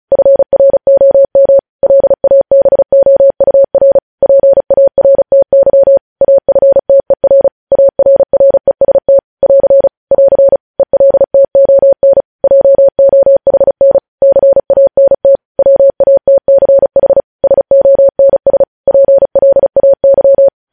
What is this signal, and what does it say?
Morse code, heard as a keyed tone, about 36 wpm, 550 Hz. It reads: FROM LABOUR PARTY AFTER ARREST <AR> <AR> ELTON JOHN CANT WATCH SONS PLAY
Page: News Headlines in Morse Code at 35 WPM